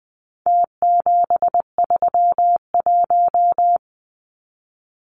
Morse code, heard as a keyed tone, about 20 wpm, 700 Hz. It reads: T731
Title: Numbers - Contest Style - 4-Digit 20wpm